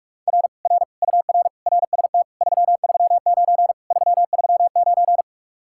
Morse code, heard as a keyed tone, about 50 wpm, 700 Hz. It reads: R R UR RST 339 339